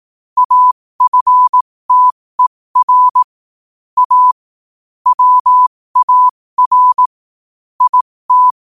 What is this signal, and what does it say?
Morse code, heard as a keyed tone, about 18 wpm, spaced as for 15 wpm, 1000 Hz. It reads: AFTER A WAR IT
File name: qotdfiles / 15wpm